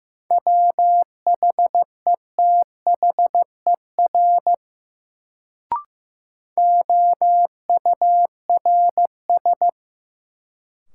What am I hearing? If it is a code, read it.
Morse code, heard as a keyed tone, about 15 wpm, 700 Hz. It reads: WHETHER OURS
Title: Sentences from Top 1300 Words - New Word Drill Alt - 15wpm